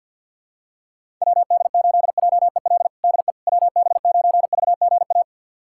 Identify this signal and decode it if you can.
Morse code, heard as a keyed tone, about 50 wpm, 700 Hz. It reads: WD8JEL DE WB9VGA
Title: USA Callsign DE Callsign - 6 Characters - 50wpm